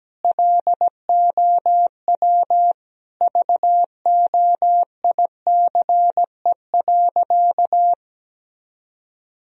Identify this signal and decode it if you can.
Morse code, heard as a keyed tone, about 17 wpm, 700 Hz. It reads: LOW VOICE.
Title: Sentences from Top 1000 Words - Review Part 1 - 17wpm